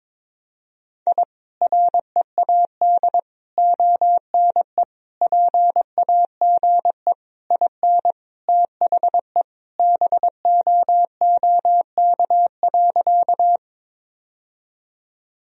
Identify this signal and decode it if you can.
Morse code, heard as a keyed tone, about 22 wpm, 700 Hz. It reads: I READ ONE PAGE IN THE BOOK.
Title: Sentences from Top 1000 Words - Review Part 1 - 22wpm